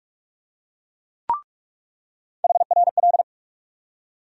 Morse code, heard as a keyed tone, about 44 wpm, 700 Hz. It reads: HRL
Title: USA Call Sign Postfixes - 3 Letters 45wpm